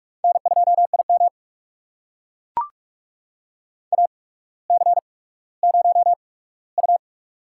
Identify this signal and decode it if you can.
Morse code, heard as a keyed tone, about 44 wpm, spaced as for 19 wpm, 700 Hz. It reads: N2IM AC0U
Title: USA Call Signs - 8x Character Spacing - 45wpm